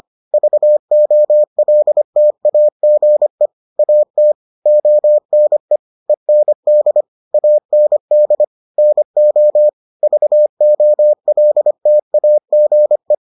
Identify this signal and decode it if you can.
Morse code, heard as a keyed tone, about 25 wpm, 600 Hz. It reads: VOLTAGE AT ONE END AND NO VOLTAGE